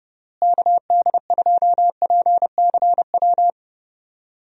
Morse code, heard as a keyed tone, about 30 wpm, 700 Hz. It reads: KD2PCW
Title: USA Call Signs - 6 Characters - Repeated 2x - 30wpm